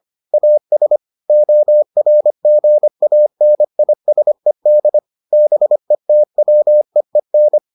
Morse code, heard as a keyed tone, about 25 wpm, 600 Hz. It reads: AS ORGANISED BETWEEN